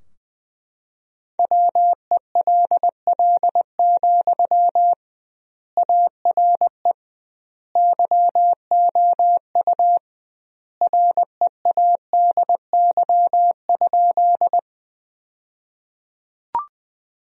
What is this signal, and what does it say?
Morse code, heard as a keyed tone, about 21 wpm, 700 Hz. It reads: WELL, ARE YOU READY?
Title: 2x Word-Spacing - Sentences from Top 400 Words - 20wpm